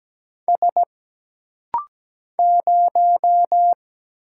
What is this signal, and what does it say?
Morse code, heard as a keyed tone, about 17 wpm, 700 Hz. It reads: S 0
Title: Single Letter-Number - Rapid-Fire 17wpm